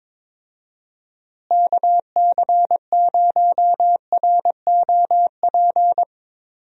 Morse code, heard as a keyed tone, about 22 wpm, 700 Hz. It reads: KC0ROP
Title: USA Call Signs - 6 Characters - Repeated 2x - 22wpm